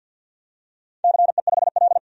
Morse code, heard as a keyed tone, about 53 wpm, 700 Hz. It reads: CE5L
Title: DXCC-100 Call Signs - 4 Characters - 50wpm